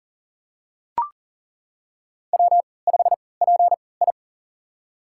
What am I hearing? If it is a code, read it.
Morse code, heard as a keyed tone, about 40 wpm, spaced as for 23 wpm, 700 Hz. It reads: W5PI
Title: USA Call Signs - 4 Characters - 3x Character Spacing - 40wpm